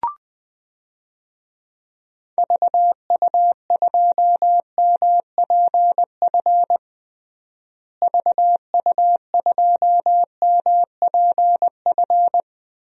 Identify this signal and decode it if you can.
Morse code, heard as a keyed tone, about 20 wpm, 700 Hz. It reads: VU2MPF VU2MPF
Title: DXCC-100 Call Signs - 6 Characters - Repeated 2x - 20wpm